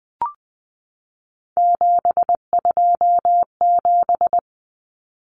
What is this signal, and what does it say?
Morse code, heard as a keyed tone, about 20 wpm, 700 Hz. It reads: 727
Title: Numbers - 3-Digit 20wpm